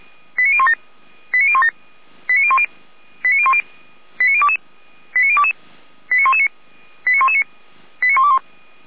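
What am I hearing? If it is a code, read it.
Five-tone selective call (ZVEI), heard as quick runs of five tones, tones of about 70 ms. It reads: 89017 89017 89010 89010 89022 89022 89119 89119 89121
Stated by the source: ZVEI five-tone call received by SDR.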